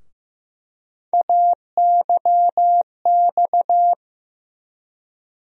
Morse code, heard as a keyed tone, about 15 wpm, 700 Hz. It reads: AYX
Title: USA Call Sign Postfixes - 3 Letters 15wpm